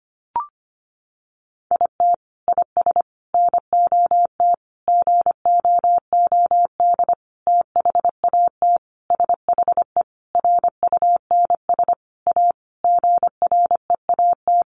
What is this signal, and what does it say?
Morse code, heard as a keyed tone, about 25 wpm, 700 Hz. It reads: IT IS NOT GOOD THAT SHE RUNS A GREAT
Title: Sentences from Top 600 Words 25wpm